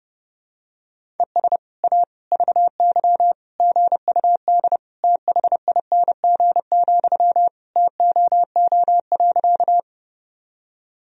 Morse code, heard as a keyed tone, about 30 wpm, 700 Hz. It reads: ES A VY GUD THING, TOO.
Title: Sentences from Top 200 Words and Abbreviations 30wpm